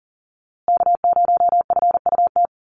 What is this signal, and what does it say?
Morse code, heard as a keyed tone, about 40 wpm, 700 Hz. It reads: K0FUT
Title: USA Call Signs Repeated 3x 40wpm